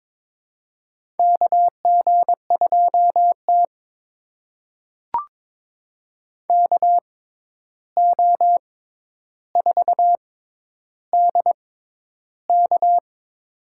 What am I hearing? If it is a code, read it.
Morse code, heard as a keyed tone, about 22 wpm, spaced as for 11 wpm, 700 Hz. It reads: KG2T KO4DK
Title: USA Call Signs - 6x Character Spacing - 22wpm